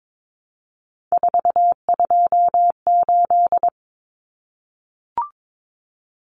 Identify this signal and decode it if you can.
Morse code, heard as a keyed tone, about 22 wpm, 700 Hz. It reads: 428
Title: Common Cut Numbers - 3-Digit 22wpm